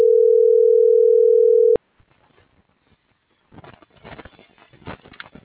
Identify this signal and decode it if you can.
Ringback tone (North America), 440 + 480 Hz.